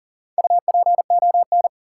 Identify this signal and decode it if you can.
Morse code, heard as a keyed tone, about 39 wpm, 700 Hz. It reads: UPON
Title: Sentences from Top 1100 Words - New Word Drill Alt - 40wpm